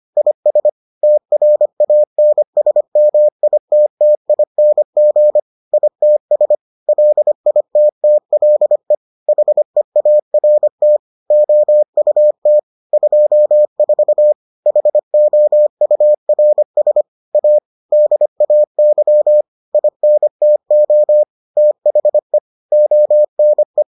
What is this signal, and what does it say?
Morse code, heard as a keyed tone, about 25 wpm, 600 Hz. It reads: IS TRANSMITTING ITS LITTLE HEART OUT 24 HOURS A DAY INTO THE ONE